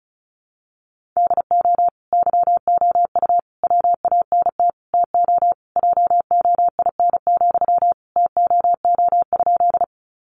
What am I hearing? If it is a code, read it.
Morse code, heard as a keyed tone, about 34 wpm, 700 Hz. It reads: DO YOU WANT TO JOIN, TOO?
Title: Sentences from Top 1400 Words 35wpm